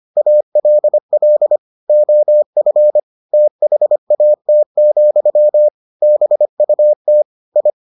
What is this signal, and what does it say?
Morse code, heard as a keyed tone, about 25 wpm, 600 Hz. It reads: ALL OF THAT, BUT I